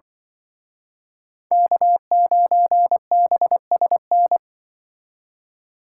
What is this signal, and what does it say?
Morse code, heard as a keyed tone, about 24 wpm, 700 Hz. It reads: K9BSN
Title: Speed Racing - USA 1x3 Call Signs 20wpm